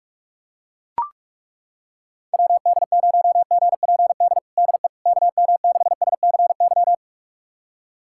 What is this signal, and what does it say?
Morse code, heard as a keyed tone, about 41 wpm, 700 Hz. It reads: WD0GPD DE KM6SCY
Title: USA Callsign DE Callsign 45wpm